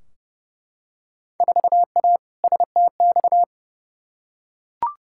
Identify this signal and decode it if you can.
Morse code, heard as a keyed tone, about 30 wpm, 700 Hz. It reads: 4A STX
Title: Field Day Exchanges Encore - 30wpm